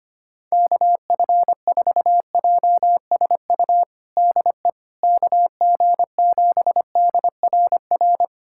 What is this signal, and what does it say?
Morse code, heard as a keyed tone, about 25 wpm, 700 Hz. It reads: KF4JSU DE KG7DRR
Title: USA Callsign DE Callsign 25wpm